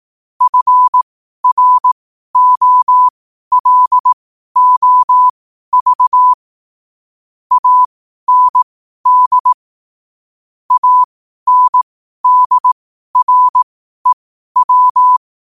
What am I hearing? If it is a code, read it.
Morse code, heard as a keyed tone, about 18 wpm, spaced as for 12 wpm, 1000 Hz. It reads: FROLOV AND ANDREW
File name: qotdfiles / 12wpm